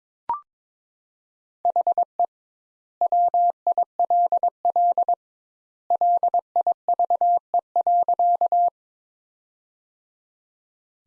Morse code, heard as a keyed tone, about 22 wpm, 700 Hz. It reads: HE WILL LIVE.
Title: Sentences Top 200 Words - 22wpm